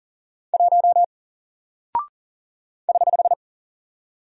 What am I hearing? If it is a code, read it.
Morse code, heard as a keyed tone, about 40 wpm, 700 Hz. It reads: 1 <HH>
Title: Single Character (Letter:Number:Punctuation:Prosign) Alt 40wpm